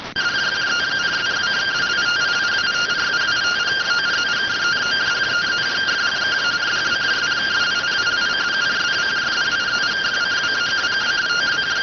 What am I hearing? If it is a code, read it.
Two-tone teleprinter signal, about 170 Hz shift.